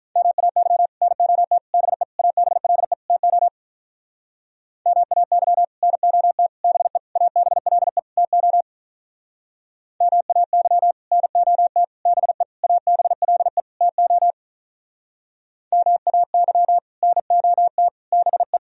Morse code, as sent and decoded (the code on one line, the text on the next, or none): -- .- -.-- / -. --- - / -... / .- -... .-.. / - --- / -- .- -.-- / -. --- - / -... / .- -... .-.. / - --- / -- .- -.-- / -. --- - / -... / .- -... .-.. / - --- / -- .- -.-- / -. --- - / -... .
MAY NOT BE ABLE TO MAY NOT BE ABLE TO MAY NOT BE ABLE TO MAY NOT BE